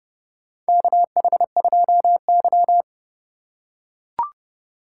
Morse code, heard as a keyed tone, about 30 wpm, 700 Hz. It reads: KH2Y
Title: USA Call Signs - 4 Characters - 3x Character Spacing - 30wpm